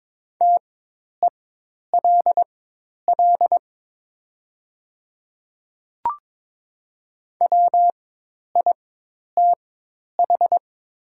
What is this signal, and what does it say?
Morse code, heard as a keyed tone, about 22 wpm, spaced as for 10 wpm, 700 Hz. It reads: TELL WITH
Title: 4x Character Spacing - Top 200 Words - 22wpm